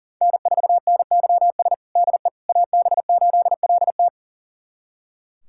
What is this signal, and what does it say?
Morse code, heard as a keyed tone, about 38 wpm, 700 Hz. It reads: N4NYS DE AB8LT
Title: USA Callsign DE Callsign - 5 Characters - 40wpm